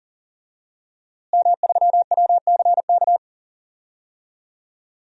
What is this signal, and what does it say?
Morse code, heard as a keyed tone, about 38 wpm, 700 Hz. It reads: M3WCK
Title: DXCC-100 Call Signs - 5 Characters - Repeated 2x - 40wpm